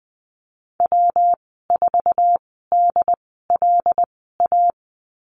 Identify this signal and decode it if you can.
Morse code, heard as a keyed tone, about 20 wpm, spaced as for 14 wpm, 700 Hz. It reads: W4DLA
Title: 161 Lesson 26g - Letter g - Call Signs 20wpm